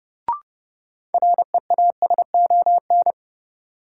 Morse code, heard as a keyed tone, about 30 wpm, 700 Hz. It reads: REASON
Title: Sets of 5 Words Encore - New Word Drill - 30wpm